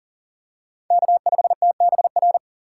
Morse code, heard as a keyed tone, about 40 wpm, 700 Hz. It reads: K5TBR
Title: USA Call Signs - 5 Characters - 3x Character Spacing - 40wpm